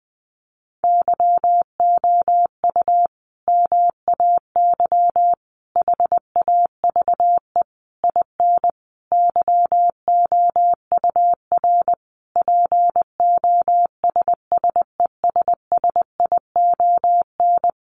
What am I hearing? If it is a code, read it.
Morse code, heard as a keyed tone, about 20 wpm, 700 Hz. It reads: YOU MAY HAVE IN YOUR POSSESSION
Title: Sentences from Top 1400 Words 20wpm